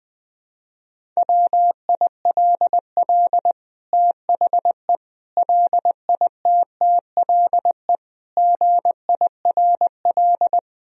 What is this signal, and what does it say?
Morse code, heard as a keyed tone, about 20 wpm, 700 Hz. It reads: WILL THE LITTLE GIRL
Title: Sentences from Top 1000 Words - Review Part 1 - 20wpm